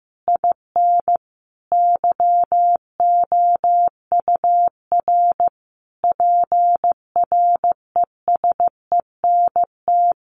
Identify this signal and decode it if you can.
Morse code, heard as a keyed tone, about 15 wpm, 700 Hz. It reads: IN YOUR PRESENT
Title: Sentences from Top 1200 Words 15wpm